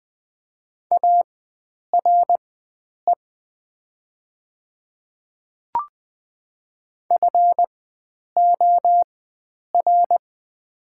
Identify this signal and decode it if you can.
Morse code, heard as a keyed tone, about 20 wpm, spaced as for 9 wpm, 700 Hz. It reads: ARE FOR
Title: 4x Character Spacing - Top 100 Words - 20wpm